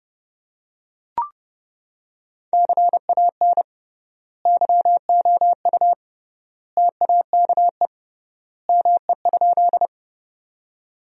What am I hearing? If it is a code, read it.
Morse code, heard as a keyed tone, about 30 wpm, 700 Hz. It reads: CAN YOU TAKE ME?
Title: 3x Word-Spacing - Sentences from Top 200 Words - 30wpm